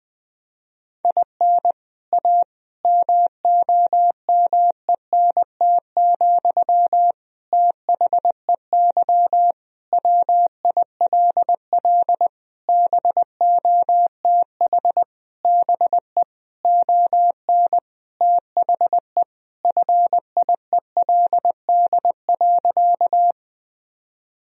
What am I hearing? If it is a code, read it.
Morse code, heard as a keyed tone, about 20 wpm, 700 Hz. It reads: IN A MOMENT, THEY WILL BOTH BE ON THE FIELD.
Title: Sentences from Top 1000 Words - Review Part 2 - 20wpm